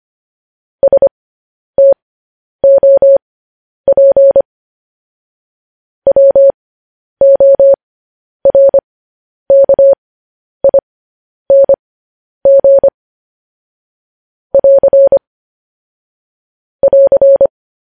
Morse code, heard as a keyed tone, about 25 wpm, spaced as for 10 wpm, 550 Hz. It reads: STOP WORKING <AR> <AR>
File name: podcast / News headlines-1x-25x10